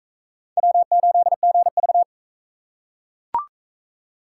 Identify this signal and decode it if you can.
Morse code, heard as a keyed tone, about 41 wpm, 700 Hz. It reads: W8GV